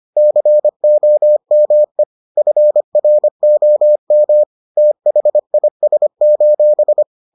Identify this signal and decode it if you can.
Morse code, heard as a keyed tone, about 25 wpm, 600 Hz. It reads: COME FROM THIS: